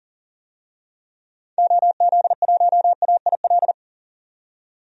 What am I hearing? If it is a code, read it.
Morse code, heard as a keyed tone, about 40 wpm, 700 Hz. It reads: OZ1AIL
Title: DXCC-100 Call Signs - 6 Characters - 40wpm